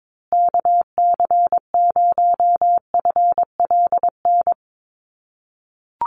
Morse code, heard as a keyed tone, about 22 wpm, 700 Hz. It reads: KC0FLN
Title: USA Call Signs Repeated 3x 22wpm